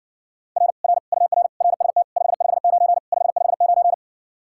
Morse code, heard as a keyed tone, about 60 wpm, 700 Hz. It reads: R R UR RST 559 559